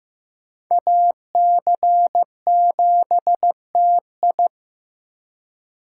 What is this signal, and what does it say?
Morse code, heard as a keyed tone, about 18 wpm, 700 Hz. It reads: AC7TI
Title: Speed Racing - USA Call Signs 15wpm